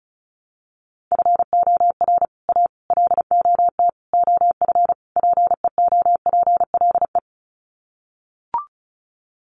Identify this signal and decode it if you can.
Morse code, heard as a keyed tone, about 34 wpm, 700 Hz. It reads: FOR A LOT OF PEOPLE
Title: Sets of 5 Words Encore 35wpm